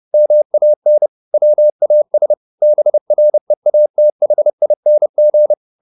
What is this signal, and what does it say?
Morse code, heard as a keyed tone, about 30 wpm, 600 Hz. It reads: MAN WAS BREATHING